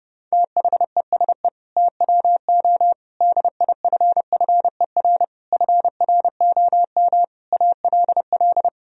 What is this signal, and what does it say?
Morse code, heard as a keyed tone, about 32 wpm, 700 Hz. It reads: THESE TWO DIFFER FROM ALL
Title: Sentences from Top 1000 Words - Review Part 1 - 30wpm